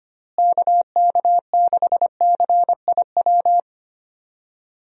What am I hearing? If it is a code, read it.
Morse code, heard as a keyed tone, about 25 wpm, 700 Hz. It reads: KK6CIW
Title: USA Call Signs - 6 Characters - Repeated 2x - 25wpm